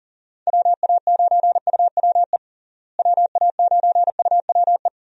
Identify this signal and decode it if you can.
Morse code, heard as a keyed tone, about 37 wpm, 700 Hz. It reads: WA9UWE WA9UWE
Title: USA Call Signs - 6 Characters - Repeated 3x - 40wpm